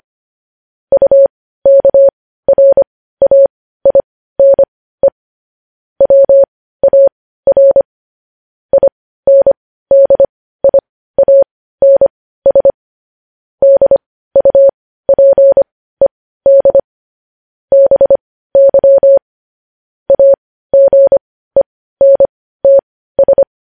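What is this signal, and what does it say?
Morse code, heard as a keyed tone, about 25 wpm, spaced as for 15 wpm, 550 Hz. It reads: UKRAINE WAR INDIANS DUPED BY AGENTS
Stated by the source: Morse code news headlines at 15.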